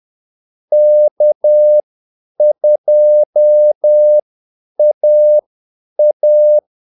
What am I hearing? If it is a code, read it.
Morse code, heard as a keyed tone, about 10 wpm, spaced as for 8 wpm, 600 Hz. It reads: K2AA
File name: Callsigns-10-8wpm